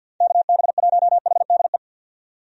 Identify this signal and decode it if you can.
Morse code, heard as a keyed tone, about 50 wpm, 700 Hz. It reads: KB1HDE
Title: USA Call Signs - 6 Characters - 50wpm